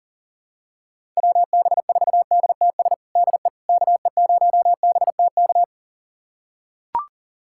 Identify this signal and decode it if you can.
Morse code, heard as a keyed tone, about 37 wpm, 700 Hz. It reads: WB4DTS DE KE0BTK
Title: USA Callsign DE Callsign - 6 Characters - 40wpm